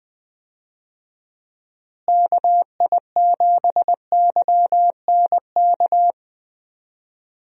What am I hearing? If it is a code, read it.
Morse code, heard as a keyed tone, about 20 wpm, 700 Hz. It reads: KI7YNK
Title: USA Call Signs - 6 Characters - Repeated 3x - 20wpm